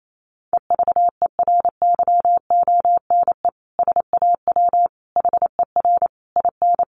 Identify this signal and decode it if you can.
Morse code, heard as a keyed tone, about 28 wpm, 700 Hz. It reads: EVERYONE SAW HER IN
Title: Sentences from Top 800 Words 28wpm